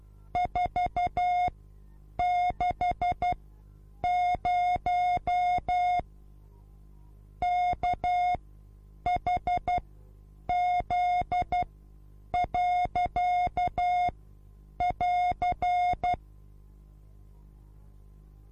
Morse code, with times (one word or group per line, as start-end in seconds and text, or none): 0.4-6.0 460
7.4-16.1 KHZ.<AR>